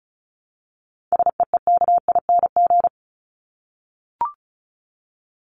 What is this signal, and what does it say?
Morse code, heard as a keyed tone, about 34 wpm, 700 Hz. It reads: SEEKING
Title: ING Suffix - Top 100 Words with Base Words 35wpm